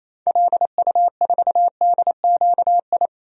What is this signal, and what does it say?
Morse code, heard as a keyed tone, about 27 wpm, 700 Hz. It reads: LU4DQI
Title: DXCC-100 Call Signs - 6 Characters - 28wpm